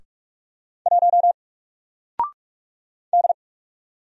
Morse code, heard as a keyed tone, about 45 wpm, 700 Hz. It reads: 1 D
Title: Single Character (Letter:Number:Punctuation:Prosign) 45wpm